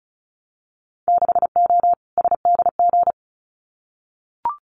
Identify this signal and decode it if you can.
Morse code, heard as a keyed tone, about 34 wpm, 700 Hz. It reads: 6O SDG
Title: Winter Field Day Exchanges Encore 35wpm